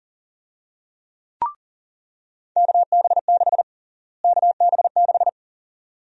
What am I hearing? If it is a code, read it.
Morse code, heard as a keyed tone, about 37 wpm, 700 Hz. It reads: KB6 KB6
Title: USA Call Sign Prefixes - Repeated 2x - 40wpm